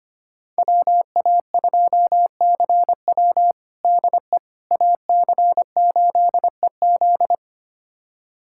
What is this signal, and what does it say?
Morse code, heard as a keyed tone, about 25 wpm, 700 Hz. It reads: WA2CW DE AC8EZ
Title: USA Callsign DE Callsign - 5 Characters - 25wpm